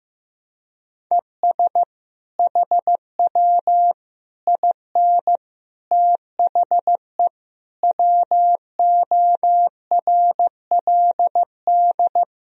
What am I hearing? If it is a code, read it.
Morse code, heard as a keyed tone, about 15 wpm, 700 Hz. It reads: ES HW IN THE WORLD
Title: Sentences from Top 200 Words and Abbreviations 15wpm